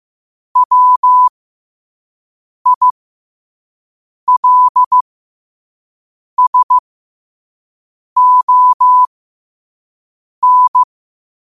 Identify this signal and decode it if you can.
Morse code, heard as a keyed tone, about 15 wpm, spaced as for 5 wpm, 1000 Hz. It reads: WILSON
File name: qotdfiles / 5wpm